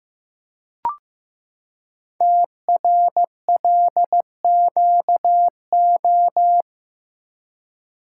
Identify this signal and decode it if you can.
Morse code, heard as a keyed tone, about 15 wpm, 700 Hz. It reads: TRLQO
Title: Code Groups - 5 Letter:Numbers - 15wpm